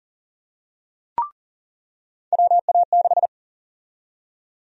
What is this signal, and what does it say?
Morse code, heard as a keyed tone, about 38 wpm, 700 Hz. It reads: WA6
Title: USA Call Sign Prefixes - 40wpm